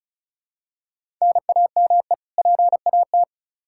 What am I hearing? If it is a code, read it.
Morse code, heard as a keyed tone, about 34 wpm, 700 Hz. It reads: NAME PAT
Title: Practice QSO - Exchange 1 - Beginning - 35wpm